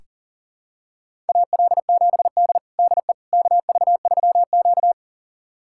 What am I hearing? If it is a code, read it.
Morse code, heard as a keyed tone, about 40 wpm, 700 Hz. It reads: AL7D DE KV3Q
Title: USA Callsign DE Callsign - 4 Characters - 40wpm